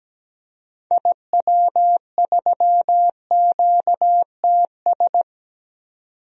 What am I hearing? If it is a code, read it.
Morse code, heard as a keyed tone, about 17 wpm, 700 Hz. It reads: IW3QTS
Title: DXCC-100 Call Signs - 6 Characters - 17wpm